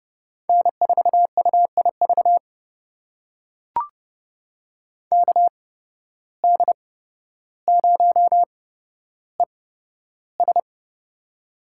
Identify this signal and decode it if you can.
Morse code, heard as a keyed tone, about 31 wpm, spaced as for 13 wpm, 700 Hz. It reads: N4UIV KD0ES
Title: USA Call Signs - 5 Characters - 8x Character Spacing - 30wpm